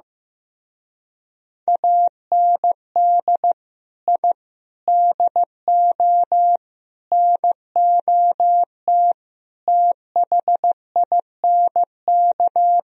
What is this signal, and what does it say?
Morse code, heard as a keyed tone, about 15 wpm, 700 Hz. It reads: AND I DO NOT THINK
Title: Speed-Racing - Sets of 5 Words 15wpm